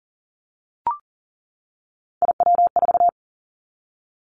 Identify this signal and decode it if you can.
Morse code, heard as a keyed tone, about 40 wpm, 700 Hz. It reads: IW4
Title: Callsign Prefixes for Top 50 DXCC - 40wpm